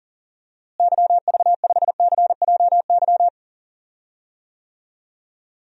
Morse code, heard as a keyed tone, about 40 wpm, 700 Hz. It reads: YV5CJY
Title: DXCC-100 Call Signs - 6 Characters - Repeated 2x - 40wpm